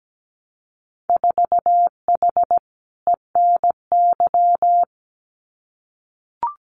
Morse code, heard as a keyed tone, about 17 wpm, 700 Hz. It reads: 4H ENY
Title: Winter Field Day Exchanges Encore 17wpm